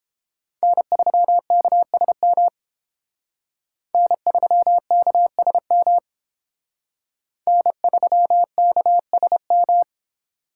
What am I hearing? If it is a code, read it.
Morse code, heard as a keyed tone, about 28 wpm, 700 Hz. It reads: N3KSM N3KSM N3KSM